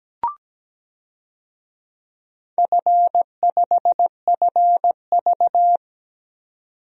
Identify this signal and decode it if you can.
Morse code, heard as a keyed tone, about 17 wpm, 700 Hz. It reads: F5FV
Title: DXCC-100 Call Signs - 4 Characters - 17wpm